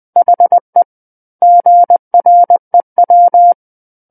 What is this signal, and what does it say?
Morse code, heard as a keyed tone, about 20 wpm, 700 Hz. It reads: HE GREW
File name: 20 WPM